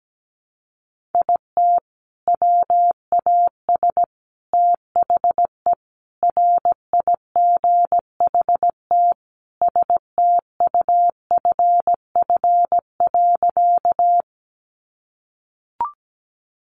Morse code, heard as a keyed tone, about 17 wpm, 700 Hz. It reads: IT WAS THE RIGHT STUFF.
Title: Sentences from Top 1400 Words 17wpm